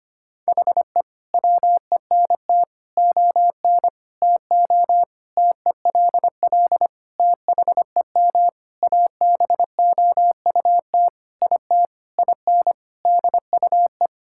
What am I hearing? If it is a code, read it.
Morse code, heard as a keyed tone, about 25 wpm, 700 Hz. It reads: HE WENT ON TO TELL THEM ABOUT IT IN DUE
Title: Sentences from Top 1000 Words - Review Part 6 - 25wpm